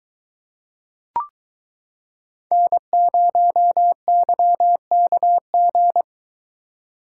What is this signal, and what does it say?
Morse code, heard as a keyed tone, about 23 wpm, 700 Hz. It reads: N0YKG